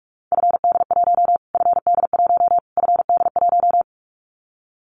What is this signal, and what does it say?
Morse code, heard as a keyed tone, about 44 wpm, 700 Hz. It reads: FD1 FD1 FD1
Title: Callsign Prefixes for Top 25 DXCC - Repeated 3x - 45wpm